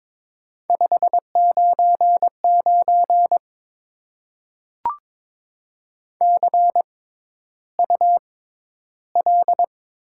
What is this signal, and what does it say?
Morse code, heard as a keyed tone, about 22 wpm, spaced as for 11 wpm, 700 Hz. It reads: 599 CUL
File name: Top 90 QSO Elements - 6x Character Spacing - 22wpm